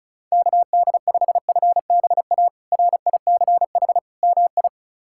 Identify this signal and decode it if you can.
Morse code, heard as a keyed tone, about 34 wpm, 700 Hz. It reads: KD5FBA RICH MI
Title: K1USN SST QSO Practice - 35wpm